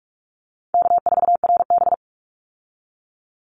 Morse code, heard as a keyed tone, about 44 wpm, 700 Hz. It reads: K4RB
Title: Easy Call Signs Repeated 3x - 45wpm